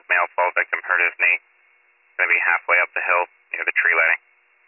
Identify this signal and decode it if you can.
Emergency Alert System (SAME) tones, heard as harsh buzzing bursts, with speech. It is heard from 0.5 to 4.2 s.